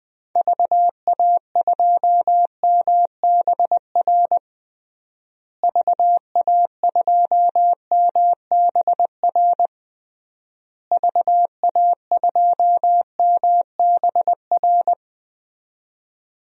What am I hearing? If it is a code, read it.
Morse code, heard as a keyed tone, about 21 wpm, 700 Hz. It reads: VA2MBR VA2MBR VA2MBR
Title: DXCC-100 Call Signs - 6 Characters - Repeated 3x - 20wpm